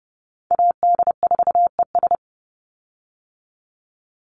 Morse code, heard as a keyed tone, about 30 wpm, 700 Hz. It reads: AD4ES
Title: Call Signs Repeated 3x - 30wpm